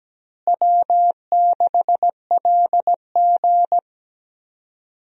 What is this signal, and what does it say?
Morse code, heard as a keyed tone, about 17 wpm, 700 Hz. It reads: W6LG
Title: USA Call Signs - 4 Characters - Repeated 3x - 17wpm